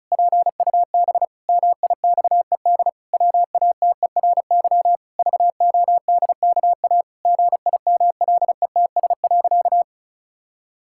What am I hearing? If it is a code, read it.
Morse code, heard as a keyed tone, about 36 wpm, 700 Hz. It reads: PUB MIXED WATERY VODKA GIMLETS.
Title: Pangram Challenge v2 - 35wpm